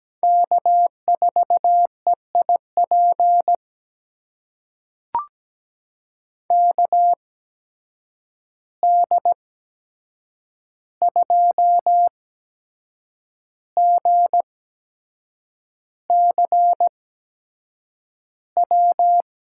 Morse code, heard as a keyed tone, about 17 wpm, spaced as for 7 wpm, 700 Hz. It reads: K4EIP KD2GCW
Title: USA Call Signs - 8x Character Spacing - 17wpm